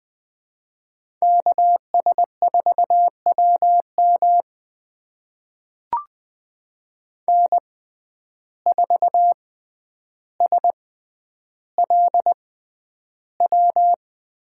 Morse code, heard as a keyed tone, about 20 wpm, spaced as for 10 wpm, 700 Hz. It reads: KS4WM N4SLW
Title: USA Call Signs - 6x Character Spacing - 20wpm